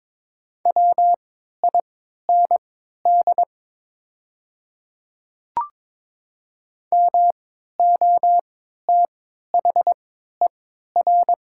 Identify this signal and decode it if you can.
Morse code, heard as a keyed tone, about 22 wpm, spaced as for 12 wpm, 700 Hz. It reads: WIND MOTHER
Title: 3x Character Spacing - Top 500 Words - 22wpm